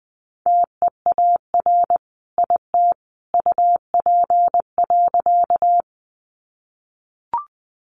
Morse code, heard as a keyed tone, about 20 wpm, 700 Hz. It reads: TEAR IT UP.
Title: A Sound Of Thunder 20wpm